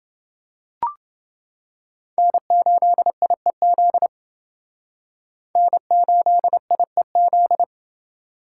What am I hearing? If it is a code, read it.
Morse code, heard as a keyed tone, about 29 wpm, 700 Hz. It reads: N8IEZ N8IEZ